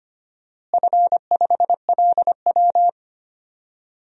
Morse code, heard as a keyed tone, about 25 wpm, 700 Hz. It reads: F5LW
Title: DXCC-100 Call Signs - 4 Characters - Repeated 3x - 25wpm